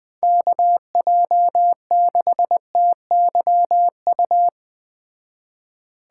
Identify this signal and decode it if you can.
Morse code, heard as a keyed tone, about 20 wpm, 700 Hz. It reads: KJ6TYU